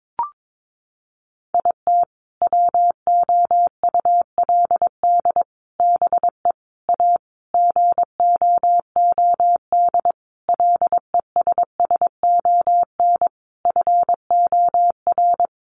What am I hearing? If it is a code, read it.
Morse code, heard as a keyed tone, about 22 wpm, 700 Hz. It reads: IT WOULD BE A GOOD LESSON FOR
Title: Sentences from Top 1200 Words 22wpm